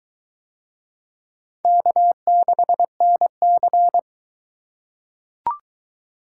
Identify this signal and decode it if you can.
Morse code, heard as a keyed tone, about 21 wpm, 700 Hz. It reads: K6NC